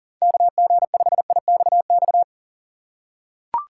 Morse code, heard as a keyed tone, about 40 wpm, 700 Hz. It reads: KG5IXX
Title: USA Call Signs - 6 Characters - 6x Character Spacing - 40wpm